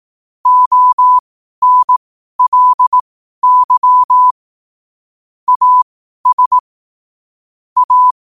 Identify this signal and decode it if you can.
Morse code, heard as a keyed tone, about 18 wpm, spaced as for 12 wpm, 1000 Hz. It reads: ONLY AS A
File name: qotdfiles / 12wpm